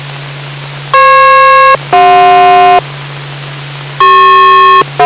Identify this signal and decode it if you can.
Aircraft SELCAL, heard as two pairs of tones. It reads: FP-CJ